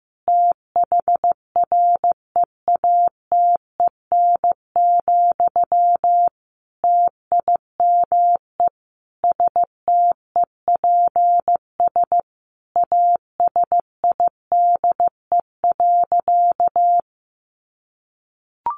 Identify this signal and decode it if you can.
Morse code, heard as a keyed tone, about 15 wpm, 700 Hz. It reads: THREATEN, TIME STEPS ASIDE.
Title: A Sound Of Thunder 15wpm